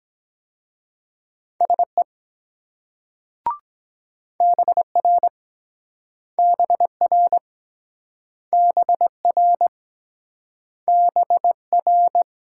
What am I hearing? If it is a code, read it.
Morse code, heard as a keyed tone, about 23 wpm, 700 Hz. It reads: SE BR BR BR BR